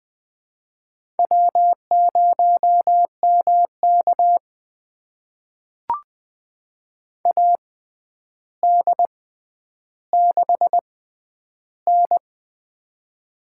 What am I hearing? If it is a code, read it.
Morse code, heard as a keyed tone, about 20 wpm, spaced as for 10 wpm, 700 Hz. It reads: W0MK AD6N
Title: USA Call Signs - 4 Characters - 6x Character Spacing - 20wpm